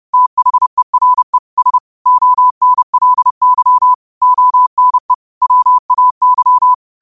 Morse code, heard as a keyed tone, about 30 wpm, 1000 Hz. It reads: THERE S ONLY ONE WAY
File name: qotdfiles / 30wpm